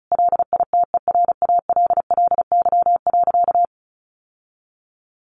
Morse code, heard as a keyed tone, about 34 wpm, 700 Hz. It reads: LITERALLY.
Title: Sentences from Top 1400 Words 35wpm